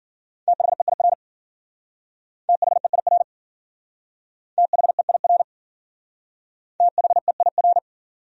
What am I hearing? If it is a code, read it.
Morse code, heard as a keyed tone, about 50 wpm, 700 Hz. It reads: THEIR THEIR THEIR THEIR
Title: Speed-Racing Top 100 Words 40wpm